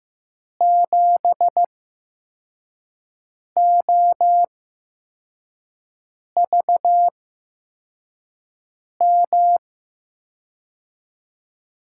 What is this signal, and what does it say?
Morse code, heard as a keyed tone, about 15 wpm, spaced as for 4 wpm, 700 Hz. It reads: 7OVM
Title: USA Call Signs - 6 Characters - 8x Character Spacing - 15wpm